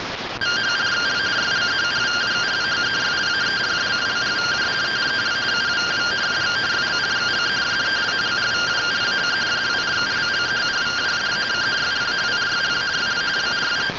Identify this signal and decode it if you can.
Two-tone teleprinter signal, about 170 Hz shift.